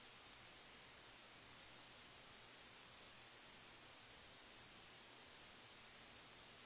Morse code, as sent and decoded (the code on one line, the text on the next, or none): none